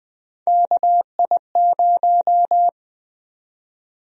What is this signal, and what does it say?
Morse code, heard as a keyed tone, about 20 wpm, 700 Hz. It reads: KI0
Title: USA Call Sign Prefixes - 20wpm